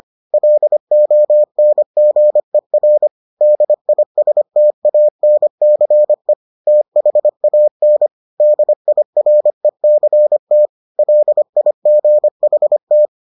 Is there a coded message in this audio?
Yes: Morse code from 0.3 to 13.1 s, heard as a keyed tone, about 25 wpm, 600 Hz. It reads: LONGER DISTANCE THAN DIRECT LIGHT